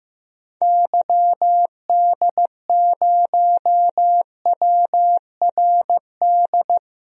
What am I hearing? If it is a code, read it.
Morse code, heard as a keyed tone, about 15 wpm, 700 Hz. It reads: YD0WRD
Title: DXCC-100 Call Signs - 6 Characters - 15wpm